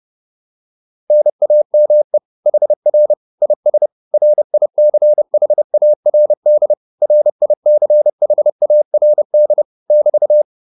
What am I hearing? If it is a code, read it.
Morse code, heard as a keyed tone, about 30 wpm, 600 Hz. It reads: NAME HR IS RICHARD RICHARD <BT>